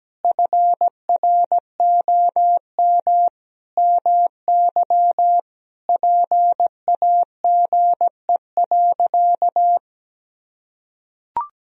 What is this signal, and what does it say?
Morse code, heard as a keyed tone, about 17 wpm, 700 Hz. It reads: FROM MY PAGE.
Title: Sentences from Top 1000 Words - Review Part 1 - 17wpm